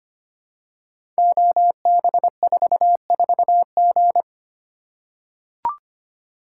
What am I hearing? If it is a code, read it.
Morse code, heard as a keyed tone, about 25 wpm, 700 Hz. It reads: OB44G
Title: Code Groups - 5 Letter:Numbers - 25wpm